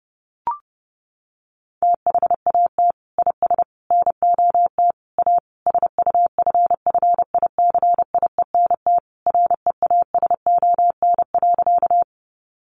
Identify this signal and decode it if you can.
Morse code, heard as a keyed tone, about 30 wpm, 700 Hz. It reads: THAT IS NOT A SUFFICIENT REASON.
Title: Sentences from Top 1400 Words 30wpm